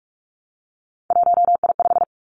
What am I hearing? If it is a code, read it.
Morse code, heard as a keyed tone, about 46 wpm, 700 Hz. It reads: 1I5
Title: Three Challenging Letter-Number Combinations 45wpm